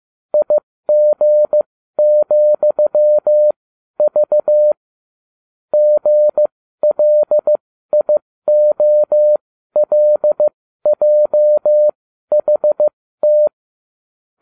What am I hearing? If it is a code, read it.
Morse code, heard as a keyed tone, about 15 wpm, spaced as for 12 wpm, 600 Hz. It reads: IG,V GLIOLJHT